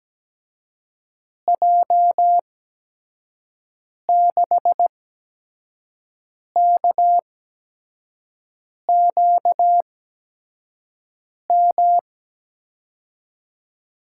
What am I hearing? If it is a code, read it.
Morse code, heard as a keyed tone, about 17 wpm, spaced as for 5 wpm, 700 Hz. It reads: J6KQM
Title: USA Call Signs - 6 Characters - 8x Character Spacing - 17wpm